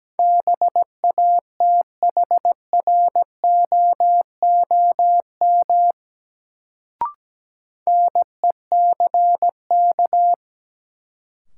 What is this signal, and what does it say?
Morse code, heard as a keyed tone, about 17 wpm, 700 Hz. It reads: BATHROOM NECK
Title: Sets of 5 Words Encore - New Word Drill - 17wpm